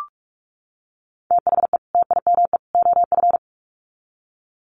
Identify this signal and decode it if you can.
Morse code, heard as a keyed tone, about 44 wpm, 700 Hz. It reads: THE TIME OF
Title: Sets of 3 Words 45wpm